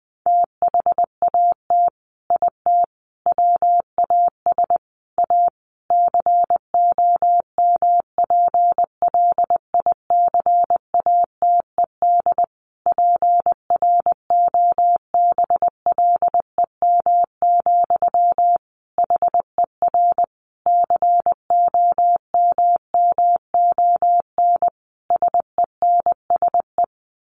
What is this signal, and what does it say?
Morse code, heard as a keyed tone, about 20 wpm, 700 Hz. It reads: THAT IT WAS A COMPLICATED PROBLEM, HER COMMON SENSE
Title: Sentences from Top 1200 Words 20wpm